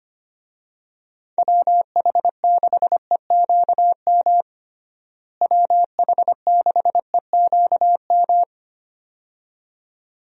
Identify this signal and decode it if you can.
Morse code, heard as a keyed tone, about 25 wpm, 700 Hz. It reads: WH6EQM WH6EQM
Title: USA Call Signs - 6 Characters - Repeated 2x - 25wpm